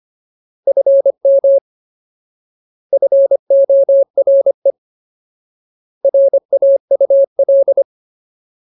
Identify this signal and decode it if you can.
Morse code, heard as a keyed tone, about 25 wpm, 550 Hz. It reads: FM FORE RAUL
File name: Ultimate Word List / 25WPM Ultimate